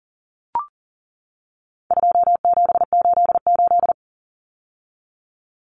Morse code, heard as a keyed tone, about 40 wpm, 700 Hz. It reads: 2788
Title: Common Cut Numbers - 4-Digit 40wpm